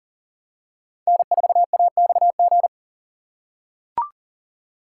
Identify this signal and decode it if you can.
Morse code, heard as a keyed tone, about 40 wpm, 700 Hz. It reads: N4AXG
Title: USA Call Signs - 5 Characters - 4x Character Spacing - 40wpm